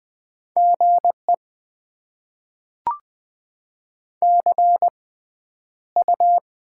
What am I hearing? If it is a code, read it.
Morse code, heard as a keyed tone, about 20 wpm, spaced as for 10 wpm, 690 Hz. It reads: GE CU
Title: Top 20 QSO Elements - 6x Character Spacing - 20wpm